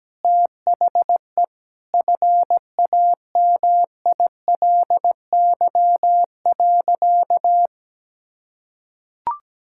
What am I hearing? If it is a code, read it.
Morse code, heard as a keyed tone, about 17 wpm, 700 Hz. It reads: THE FAMILY.
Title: Sentences from Top 1000 Words - Review Part 6 - 17wpm